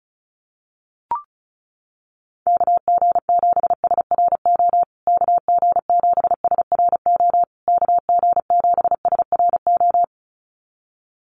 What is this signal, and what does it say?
Morse code, heard as a keyed tone, about 34 wpm, 700 Hz. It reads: KG7SRO KG7SRO KG7SRO
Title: USA Call Signs Repeated 3x 35wpm